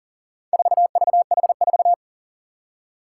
Morse code, heard as a keyed tone, about 40 wpm, 700 Hz. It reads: 4VH4
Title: Four Challenging Letter-Number Combinations 40wpm